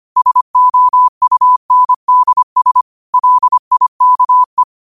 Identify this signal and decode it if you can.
Morse code, heard as a keyed tone, about 25 wpm, 1000 Hz. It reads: SOUNDS LIKE